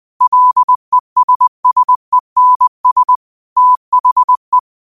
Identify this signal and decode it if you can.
Morse code, heard as a keyed tone, about 20 wpm, 1000 Hz. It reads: LESSENS THE
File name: qotdfiles / 20wpm